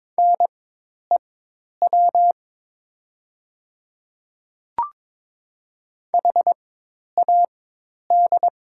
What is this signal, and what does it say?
Morse code, heard as a keyed tone, about 22 wpm, spaced as for 10 wpm, 700 Hz. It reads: NEW HAD
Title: 4x Character Spacing - Top 200 Words - 22wpm